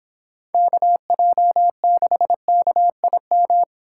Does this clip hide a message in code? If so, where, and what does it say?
Yes: Morse code from 0.5 to 3.6 s, heard as a keyed tone, about 26 wpm, 700 Hz. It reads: KJ6KIM